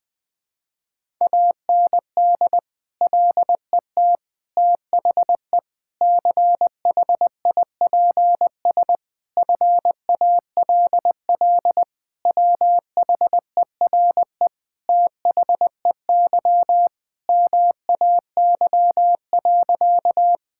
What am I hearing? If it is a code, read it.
Morse code, heard as a keyed tone, about 20 wpm, 700 Hz. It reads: AND LET THE CHIPS FALL WHERE THEY MAY.
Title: Sentences with 8-Word Phrases - 20wpm